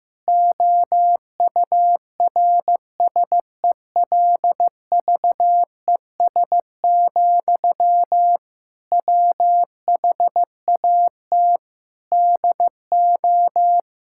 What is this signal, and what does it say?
Morse code, heard as a keyed tone, about 16 wpm, 700 Hz. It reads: OURSELVES, WHAT DO
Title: Sentences from Top 1600 Words 15wpm